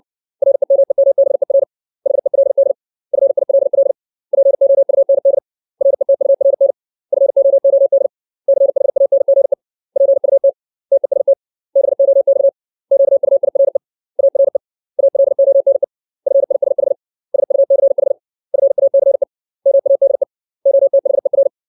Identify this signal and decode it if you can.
Morse code, heard as a keyed tone, about 60 wpm, 550 Hz. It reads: REMEMBER HELD FILL POUND RETURN FOOD CHANGE PUT TEST BOX QUIRE ARE ALONE FISH SUCH FACE MADE OTHER
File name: 632 words / 632 Words 60WPM